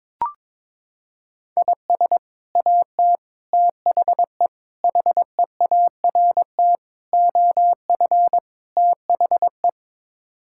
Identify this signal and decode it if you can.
Morse code, heard as a keyed tone, about 22 wpm, 700 Hz. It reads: IS AT THE HEART OF THE
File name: practice / 22wpm / Sets of 6 Words - 22wpm